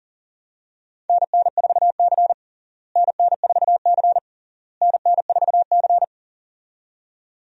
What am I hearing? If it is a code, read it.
Morse code, heard as a keyed tone, about 40 wpm, 700 Hz. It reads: NN4C NN4C NN4C
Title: USA Call Signs - 4 Characters - Repeated 3x - 40wpm